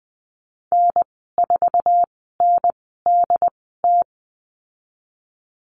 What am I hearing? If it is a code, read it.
Morse code, heard as a keyed tone, about 20 wpm, spaced as for 15 wpm, 700 Hz. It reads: N4NDT
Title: 076 Lesson 13g - Letter L - Call Signs 20wpm